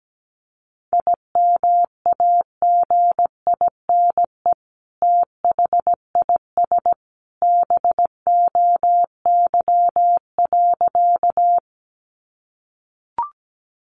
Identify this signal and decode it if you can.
Morse code, heard as a keyed tone, about 17 wpm, 700 Hz. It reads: IMAGINE THIS BOY.
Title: Sentences from Top 1400 Words 17wpm